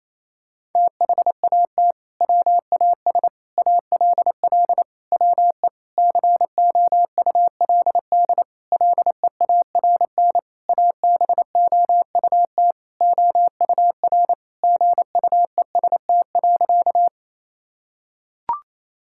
Morse code, heard as a keyed tone, about 28 wpm, 700 Hz. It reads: THAT WAS ALL WE COULD LEARN ABOUT OUR GUEST.
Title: Sentences from Top 1600 Words 28wpm